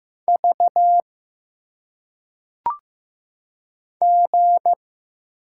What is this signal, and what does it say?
Morse code, heard as a keyed tone, about 15 wpm, 700 Hz. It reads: V G
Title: Single Letters - Phonetic Pronunciation 15wpm